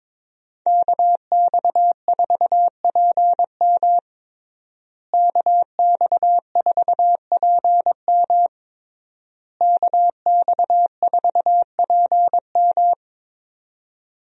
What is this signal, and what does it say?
Morse code, heard as a keyed tone, about 22 wpm, 700 Hz. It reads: KX4PM KX4PM KX4PM
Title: USA Call Signs - 5 Characters - Repeated 3x - 22wpm